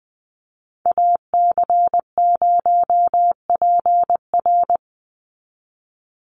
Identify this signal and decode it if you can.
Morse code, heard as a keyed tone, about 20 wpm, 700 Hz. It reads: AC0PR
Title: NA SOTA Chaser Callsigns 20wpm